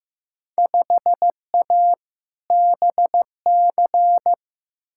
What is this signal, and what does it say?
Morse code, heard as a keyed tone, about 15 wpm, 700 Hz. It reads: 5A BC
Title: Field Day Exchanges - 15wpm